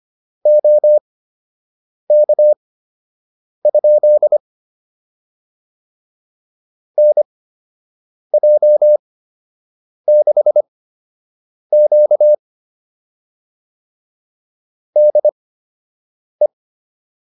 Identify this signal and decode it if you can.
Morse code, heard as a keyed tone, about 25 wpm, spaced as for 7 wpm, 600 Hz. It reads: OK? NJ6Q DE